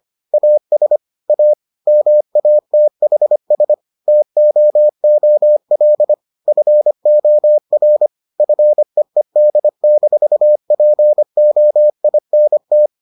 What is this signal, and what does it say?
Morse code, heard as a keyed tone, about 25 wpm, 600 Hz. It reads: AS A MATHS TOOL FOR FEED-POINT